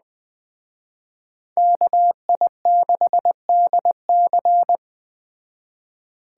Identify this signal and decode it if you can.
Morse code, heard as a keyed tone, about 20 wpm, 700 Hz. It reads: KI6DC
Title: Speed Racing - USA 2x2 Call Signs 17wpm